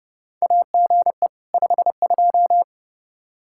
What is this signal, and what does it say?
Morse code, heard as a keyed tone, about 31 wpm, 700 Hz. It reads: AGE 52
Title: Practice QSO - Exchange 3 - More Info - 30wpm